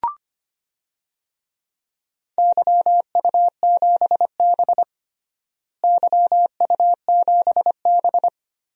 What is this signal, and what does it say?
Morse code, heard as a keyed tone, about 25 wpm, 700 Hz. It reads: YU7B YU7B
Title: DXCC-100 Call Signs - 4 Characters - Repeated 2x - 25wpm